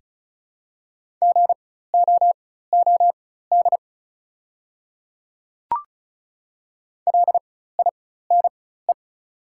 Morse code, heard as a keyed tone, about 36 wpm, spaced as for 16 wpm, 700 Hz. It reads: GOOD LINE
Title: 4x Character Spacing - Top 200 Words - 35wpm